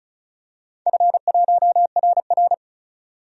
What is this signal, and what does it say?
Morse code, heard as a keyed tone, about 36 wpm, 700 Hz. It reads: F1RR
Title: DXCC-100 Call Signs - 4 Characters - 35wpm